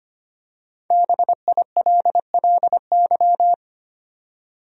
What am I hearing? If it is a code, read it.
Morse code, heard as a keyed tone, about 25 wpm, 700 Hz. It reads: BILLY
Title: USA Common Names - 25wpm